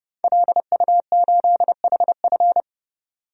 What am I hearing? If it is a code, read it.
Morse code, heard as a keyed tone, about 30 wpm, 700 Hz. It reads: LU8HF
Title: DXCC-100 Call Signs - 5 Characters - 30wpm